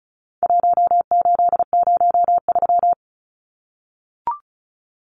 Morse code, heard as a keyed tone, about 34 wpm, 700 Hz. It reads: 1803
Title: Numbers - 4-Digit 35wpm